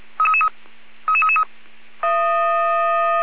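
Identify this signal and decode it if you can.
Five-tone selective call (ZVEI), heard as quick runs of five tones, tones of about 70 ms. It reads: 33333
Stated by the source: ZVEI five-tone call received by SDR.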